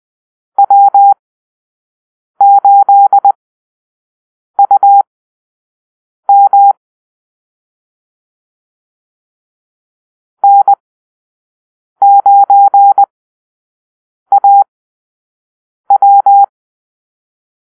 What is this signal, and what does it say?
Morse code, heard as a keyed tone, about 20 wpm, spaced as for 6 wpm, 800 Hz. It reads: W8UM N9AW
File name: arrl / Call-Sign Drill-5